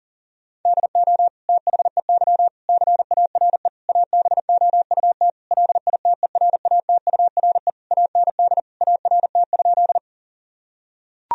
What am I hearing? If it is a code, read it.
Morse code, heard as a keyed tone, about 40 wpm, 700 Hz. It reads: DO THEY CARE ABOUT LITERATURE AND ART?
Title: Sentences from Top 1000 Words - Review Part 3 - 40wpm